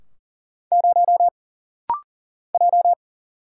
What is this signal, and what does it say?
Morse code, heard as a keyed tone, about 40 wpm, 700 Hz. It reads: 0 J
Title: Single Letter-Number Alt - Rapid-Fire 40wpm